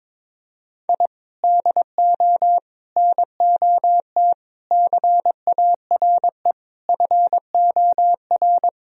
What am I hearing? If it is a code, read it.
Morse code, heard as a keyed tone, about 22 wpm, 700 Hz. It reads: I DO NOT CARE FOR
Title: Sentences from Top 1000 Words - Review Part 3 - 22wpm